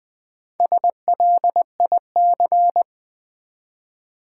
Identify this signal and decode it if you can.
Morse code, heard as a keyed tone, about 20 wpm, 700 Hz. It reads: SLIC
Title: Build A Word - 1 Letter At A Time - 3 to 5-Letter Words - 20wpm